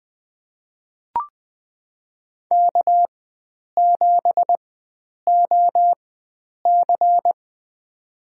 Morse code, heard as a keyed tone, about 20 wpm, spaced as for 9 wpm, 700 Hz. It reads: K7OC
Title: USA Call Signs - 4 Characters - 4x Character Spacing - 20wpm